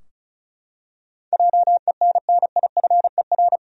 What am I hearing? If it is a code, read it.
Morse code, heard as a keyed tone, about 36 wpm, 700 Hz. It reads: JENNIFER
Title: USA Common Names - 35wpm